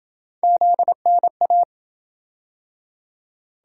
Morse code, heard as a keyed tone, about 26 wpm, 700 Hz. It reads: ZNA